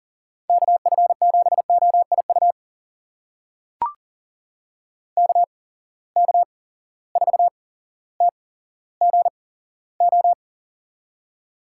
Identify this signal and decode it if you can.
Morse code, heard as a keyed tone, about 40 wpm, spaced as for 17 wpm, 700 Hz. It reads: KF7OIU KK4TGO
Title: USA Call Signs - 6 Characters - 8x Character Spacing - 40wpm